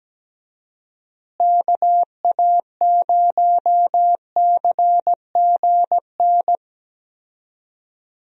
Morse code, heard as a keyed tone, about 17 wpm, 700 Hz. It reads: KA0CGN
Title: USA Call Signs - 6 Characters - Repeated 3x - 17wpm